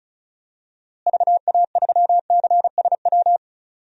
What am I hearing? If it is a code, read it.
Morse code, heard as a keyed tone, about 34 wpm, 700 Hz. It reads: VA3CSW
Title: DXCC-100 Call Signs - 6 Characters - Repeated 2x - 35wpm